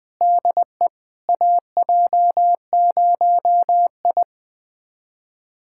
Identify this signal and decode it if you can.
Morse code, heard as a keyed tone, about 21 wpm, 700 Hz. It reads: DE AJ0I
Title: USA Callsign DE Callsign - 4 Characters - 20wpm